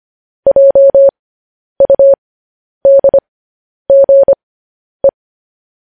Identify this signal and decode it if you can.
Morse code, heard as a keyed tone, about 25 wpm, spaced as for 10 wpm, 550 Hz. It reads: JUDGE
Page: news headlines in morse code at 10 wpm